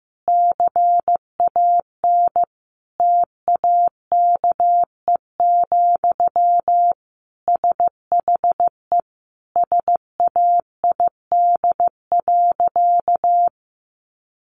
Morse code, heard as a keyed tone, about 15 wpm, 700 Hz. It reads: CAN TAKE, SHE SAID.
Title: Sentences from Top 1500 Words 15wpm